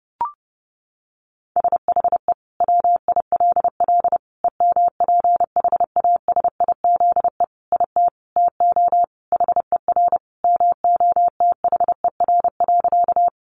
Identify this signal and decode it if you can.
Morse code, heard as a keyed tone, about 30 wpm, 700 Hz. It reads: SHE WILL EMPHASIZE IT TO HER MOTHER.
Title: Sentences from Top 1500 Words 30wpm